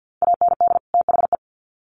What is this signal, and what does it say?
Morse code, heard as a keyed tone, about 50 wpm, 700 Hz. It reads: AND THE
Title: Sets of 2 Words 50wpm